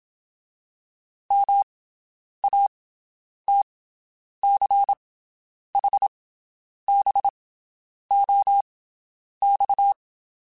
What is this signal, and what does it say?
Morse code, heard as a keyed tone, about 27 wpm, spaced as for 9 wpm, 800 Hz. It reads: MATCHBOX
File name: QSOs Spaced